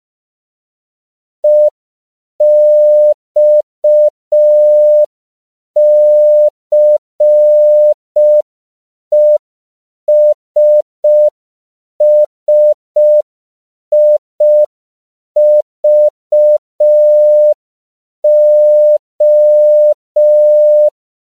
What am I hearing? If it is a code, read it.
Morse code, heard as a keyed tone, about 5 wpm, 600 Hz. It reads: EXCESSIVO
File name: CW / 5PPM